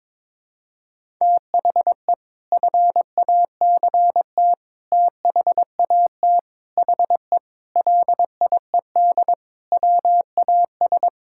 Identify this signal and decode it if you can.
Morse code, heard as a keyed tone, about 22 wpm, 700 Hz. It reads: THE FACT THAT HE LIED WAS
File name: practice / 22wpm / Sentences with 4-Word Phrases - 22wpm